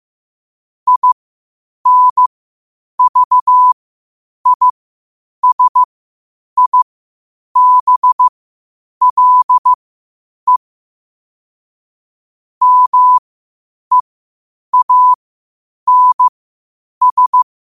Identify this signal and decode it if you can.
Morse code, heard as a keyed tone, about 15 wpm, spaced as for 8 wpm, 1000 Hz. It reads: INVISIBLE MEANS